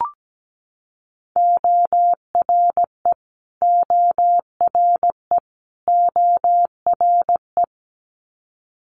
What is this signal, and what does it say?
Morse code, heard as a keyed tone, about 17 wpm, 700 Hz. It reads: ORE ORE ORE
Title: Multi-letter Phonemes Repeated 3x 17wpm